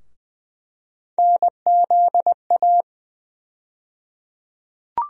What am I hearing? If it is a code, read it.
Morse code, heard as a keyed tone, about 20 wpm, 700 Hz. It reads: NZA
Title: USA Call Sign Postfixes - 3 Letters 20wpm